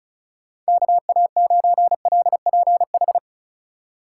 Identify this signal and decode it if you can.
Morse code, heard as a keyed tone, about 36 wpm, 700 Hz. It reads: KA9LPH
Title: USA Call Signs - 6 Characters - 4x Character Spacing - 35wpm